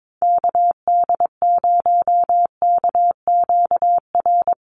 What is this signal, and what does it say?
Morse code, heard as a keyed tone, about 22 wpm, 700 Hz. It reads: KD0KQR
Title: USA Call Signs Repeated 2x 22wpm